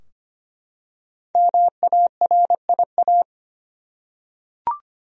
Morse code, heard as a keyed tone, about 25 wpm, 700 Hz. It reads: MARIA
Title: USA Common Names - 25wpm